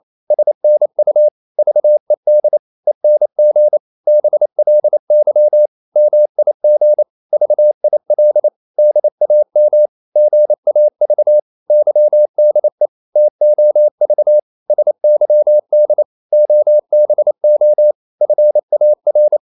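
Morse code, heard as a keyed tone, about 28 wpm, 600 Hz. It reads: SNU VED ENG BLY MIG VIL DAM GAV YDE TOV SYD OBO FAR
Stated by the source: Danish short words sent in Morse at 28 wpm, from CW med Gnister.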